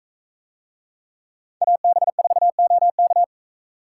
Morse code, heard as a keyed tone, about 42 wpm, 700 Hz. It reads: AB4OK